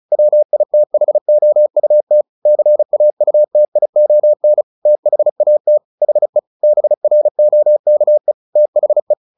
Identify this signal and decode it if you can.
Morse code, heard as a keyed tone, about 34 wpm, 600 Hz. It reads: WITHOUT CAUTION THAT HE BROKE THE
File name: the old man and the sea 2 35wpm